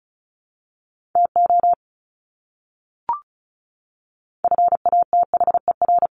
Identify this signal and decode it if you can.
Morse code, heard as a keyed tone, about 34 wpm, 700 Hz. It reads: TO FATHER
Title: Top 300 Words Repeated 35wpm